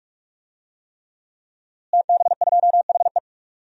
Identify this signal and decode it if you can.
Morse code, heard as a keyed tone, about 46 wpm, 700 Hz. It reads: TB2HE
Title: DXCC-100 Call Signs - 5 Characters - Repeated 3x - 45wpm